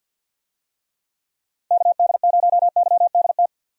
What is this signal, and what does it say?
Morse code, heard as a keyed tone, about 51 wpm, 700 Hz. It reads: KD0YDT
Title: USA Call Signs - 6 Characters - 50wpm